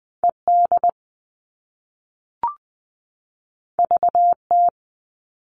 Morse code, heard as a keyed tone, about 20 wpm, 700 Hz. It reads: ED VT
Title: All Cut Numbers - 2-Digit 20wpm